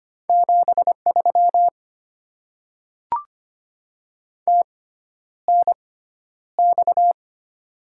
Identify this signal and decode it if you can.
Morse code, heard as a keyed tone, about 25 wpm, spaced as for 11 wpm, 690 Hz. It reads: 73 TNX
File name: Top 20 QSO Elements - 6x Character Spacing - 25wpm